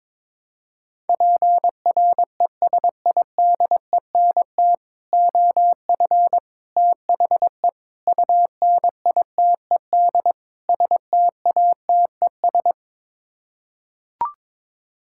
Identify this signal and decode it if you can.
Morse code, heard as a keyed tone, about 22 wpm, 700 Hz. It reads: PRESIDENT OF THE UNITED STATES
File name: practice / 22wpm / Build A Phrase - 5-Word Phrases - 22wpm